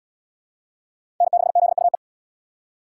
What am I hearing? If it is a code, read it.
Morse code, heard as a keyed tone, about 76 wpm, 700 Hz. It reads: N6ZUE